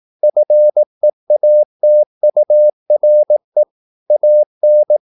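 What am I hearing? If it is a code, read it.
Morse code, heard as a keyed tone, about 18 wpm, 600 Hz. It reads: FEATURE AN
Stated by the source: Morse practice at 18 wpm, English text, from CW med Gnister.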